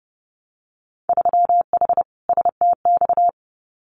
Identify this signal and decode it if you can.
Morse code, heard as a keyed tone, about 30 wpm, 700 Hz. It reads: 3H STX
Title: Winter Field Day Exchanges Encore 30wpm